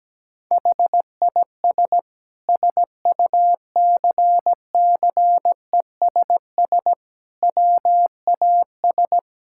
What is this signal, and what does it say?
Morse code, heard as a keyed tone, about 17 wpm, 700 Hz. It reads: HIS SUCCESS WAS
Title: Sentences from Top 1000 Words - Review Part 6 - 17wpm